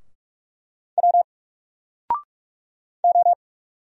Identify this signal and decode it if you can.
Morse code, heard as a keyed tone, about 44 wpm, 700 Hz. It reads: W O
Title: Single Letter-Number 45wpm